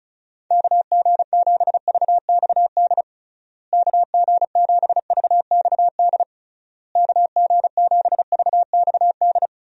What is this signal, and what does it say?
Morse code, heard as a keyed tone, about 34 wpm, 700 Hz. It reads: KG7VXD KG7VXD KG7VXD
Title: USA Call Signs - 6 Characters - Repeated 3x - 35wpm